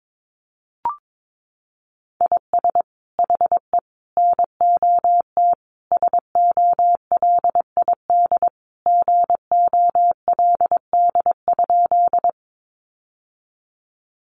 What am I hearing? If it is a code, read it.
Morse code, heard as a keyed tone, about 22 wpm, 700 Hz. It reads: IS HE NOT SOLID GOLD?
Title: Sentences from Top 1500 Words 22wpm